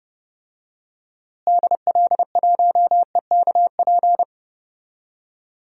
Morse code, heard as a keyed tone, about 31 wpm, 700 Hz. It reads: DL1EKP
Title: DXCC-100 Call Signs - 6 Characters - Repeated 2x - 30wpm